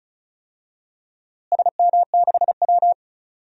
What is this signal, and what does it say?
Morse code, heard as a keyed tone, about 34 wpm, 700 Hz. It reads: SM6W
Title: DXCC-100 Call Signs - 4 Characters - Repeated 3x - 35wpm